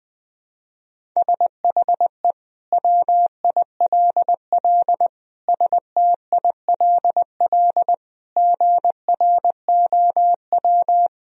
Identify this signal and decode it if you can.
Morse code, heard as a keyed tone, about 21 wpm, 700 Hz. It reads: SHE WILL STILL GROW
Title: Sentences from Top 1000 Words - Review Part 1 - 20wpm